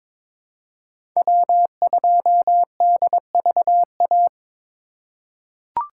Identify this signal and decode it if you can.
Morse code, heard as a keyed tone, about 22 wpm, 700 Hz. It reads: W2DVA
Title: USA Call Signs - 5 Characters - 3x Character Spacing - 22wpm